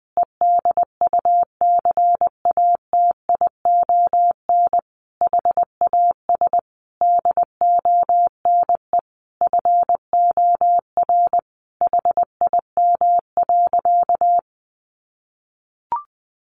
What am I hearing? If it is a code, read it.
Morse code, heard as a keyed tone, about 20 wpm, 700 Hz. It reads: EDUCATION HAS DONE FOR HIM.
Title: Sentences from Top 1400 Words 20wpm